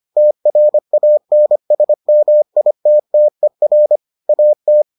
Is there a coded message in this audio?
Yes: Morse code from 0.2 to 4.8 s, heard as a keyed tone, about 25 wpm, 600 Hz. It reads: TRANSMITTER AT